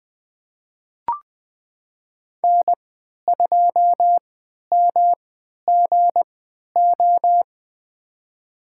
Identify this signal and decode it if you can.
Morse code, heard as a keyed tone, about 20 wpm, spaced as for 11 wpm, 700 Hz. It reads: N2MGO
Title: USA Call Signs - 5 Characters - 3x Character Spacing - 20wpm